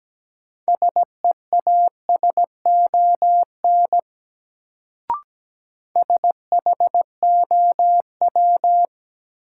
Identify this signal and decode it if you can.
Morse code, heard as a keyed tone, about 17 wpm, 700 Hz. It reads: SEASON SHOW
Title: Sets of 5 Words - New Word Drill - 17wpm